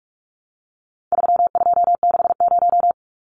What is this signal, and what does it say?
Morse code, heard as a keyed tone, about 44 wpm, 700 Hz. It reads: 3260
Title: Numbers - 4-Digit 45wpm